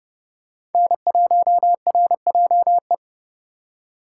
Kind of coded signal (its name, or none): Morse code